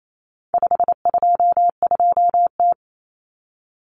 Morse code, heard as a keyed tone, about 28 wpm, 700 Hz. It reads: 522T
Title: Common Cut Numbers - 4-Digit 28wpm